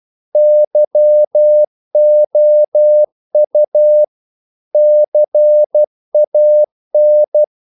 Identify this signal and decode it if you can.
Morse code, heard as a keyed tone, about 12 wpm, 600 Hz. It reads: YOU CAN
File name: CW / Practice / If12